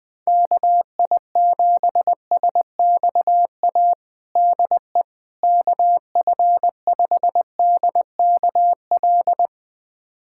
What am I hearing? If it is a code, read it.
Morse code, heard as a keyed tone, about 21 wpm, 700 Hz. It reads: KI7SXA DE KF5DKL
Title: USA Callsign DE Callsign - 6 Characters - 20wpm